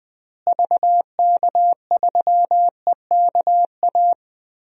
Morse code, heard as a keyed tone, about 20 wpm, 700 Hz. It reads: VK3EKA
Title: DXCC-100 Call Signs - 6 Characters - Repeated 2x - 20wpm